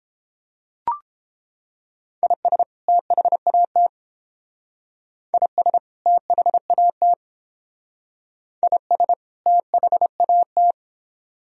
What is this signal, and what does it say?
Morse code, heard as a keyed tone, about 30 wpm, 700 Hz. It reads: IS THAT IS THAT IS THAT